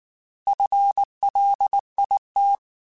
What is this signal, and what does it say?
Morse code, heard as a keyed tone, about 19 wpm, 800 Hz. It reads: FLIT
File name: FL / FL mp3s / FL Prefixes 3x2 sp @18